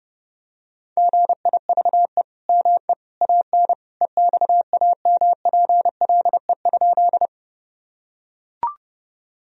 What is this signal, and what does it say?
Morse code, heard as a keyed tone, about 32 wpm, 700 Hz. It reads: GIVE ME AN EXAMPLE?
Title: Sentences from Top 1000 Words - Review Part 1 - 30wpm